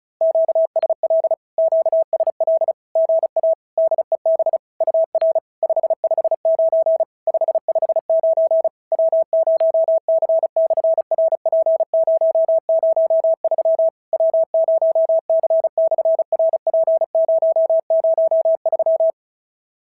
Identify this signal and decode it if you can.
Morse code, heard as a keyed tone, about 36 wpm, 650 Hz. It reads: QSL QSL GA DEB UR 559 559 W0C/RP003 W0C/RP003
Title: SOTA QSO Practice 35wpm